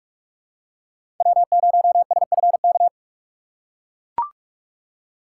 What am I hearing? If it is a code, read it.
Morse code, heard as a keyed tone, about 44 wpm, 700 Hz. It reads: W0SFK
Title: USA Call Signs - 5 Characters - 6x Character Spacing - 45wpm